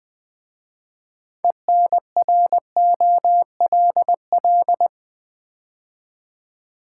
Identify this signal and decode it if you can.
Morse code, heard as a keyed tone, about 20 wpm, 700 Hz. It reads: ENROLL
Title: 6 Letter Words - First 3 Letters Given - 20wpm